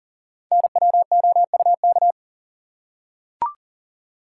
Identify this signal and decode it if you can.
Morse code, heard as a keyed tone, about 40 wpm, 700 Hz. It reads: NWOUK
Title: Code Groups - 5 Letters - 40wpm